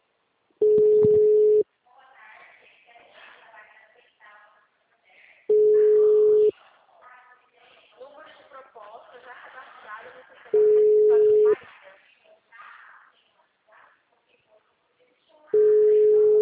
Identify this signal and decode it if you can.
Ringback tone (Europe, 425 Hz), with speech.